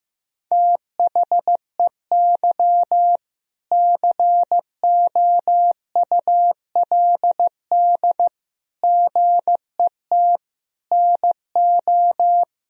Morse code, heard as a keyed tone, about 15 wpm, 700 Hz. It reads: THEY COULD GET NO
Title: Sentences from Top 1000 Words - Review Part 3 - 15wpm